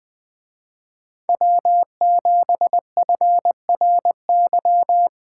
Morse code, heard as a keyed tone, about 19 wpm, 700 Hz. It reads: W7FRY
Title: Speed Racing - USA 1x3 Call Signs 17wpm